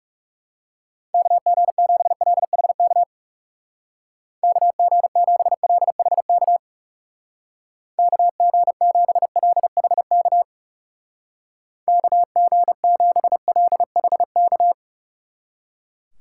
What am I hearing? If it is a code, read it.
Morse code, heard as a keyed tone, about 38 wpm, 700 Hz. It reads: KG7LHK KG7LHK KG7LHK KG7LHK